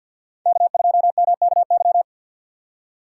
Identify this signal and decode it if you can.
Morse code, heard as a keyed tone, about 63 wpm, 700 Hz. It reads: K2MKY